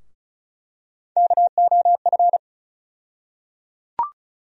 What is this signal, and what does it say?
Morse code, heard as a keyed tone, about 36 wpm, 700 Hz. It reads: KOF
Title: USA Call Sign Postfixes - 3 Letters 35wpm